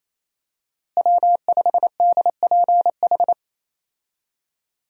Morse code, heard as a keyed tone, about 28 wpm, 700 Hz. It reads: W5DPH
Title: USA Call Signs - 5 Characters - Repeated 2x - 28wpm